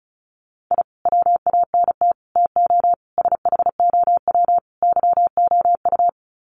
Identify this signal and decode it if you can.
Morse code, heard as a keyed tone, about 34 wpm, 700 Hz. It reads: I WANT TO SHOW YOU
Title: Sets of 5 Words 35wpm